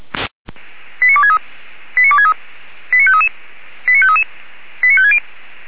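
Five-tone selective call (ZVEI), heard as quick runs of five tones, tones of about 70 ms. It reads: 89263 89263 89533 89533 89560
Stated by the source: ZVEI five-tone call 89293 received on SDR, from feuerwehr-orlamuende.org (Orlamünde fire brigade).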